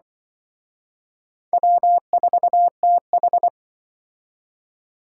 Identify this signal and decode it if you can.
Morse code, heard as a keyed tone, about 24 wpm, 700 Hz. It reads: W4TH
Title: Speed Racing - USA 1x2 Call Signs 20wpm